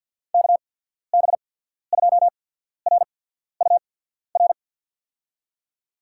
Morse code, heard as a keyed tone, about 53 wpm, spaced as for 14 wpm, 700 Hz. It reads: KB2RUR
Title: USA Call Signs - 6 Characters - 8x Character Spacing - 50wpm